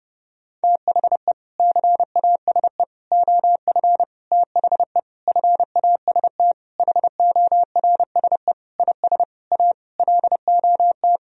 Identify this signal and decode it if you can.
Morse code, heard as a keyed tone, about 30 wpm, 700 Hz. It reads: THE CASE OF THE FAST HORSE IS A LOT
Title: Sentences from Top 1000 Words - Review Part 2 - 30wpm